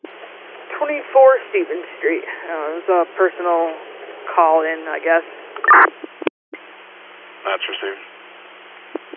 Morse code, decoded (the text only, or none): none